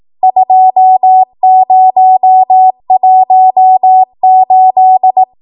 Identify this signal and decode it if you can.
Morse code, heard as a keyed tone, about 18 wpm, 750 Hz. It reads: 2018
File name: Archive / 18 WPM / 18WPM